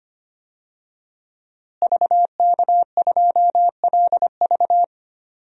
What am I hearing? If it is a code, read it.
Morse code, heard as a keyed tone, about 25 wpm, 700 Hz. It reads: VK2LV
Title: DXCC-100 Call Signs - 5 Characters - Repeated 2x - 25wpm